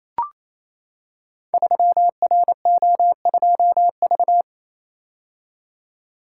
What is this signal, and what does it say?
Morse code, heard as a keyed tone, about 29 wpm, 700 Hz. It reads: 3RO2V
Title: Code Groups - 5 Letter:Numbers - 28wpm